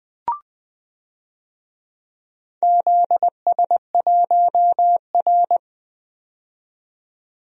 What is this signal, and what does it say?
Morse code, heard as a keyed tone, about 21 wpm, 700 Hz. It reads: ZS1R
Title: DXCC-100 Call Signs - 4 Characters - 20wpm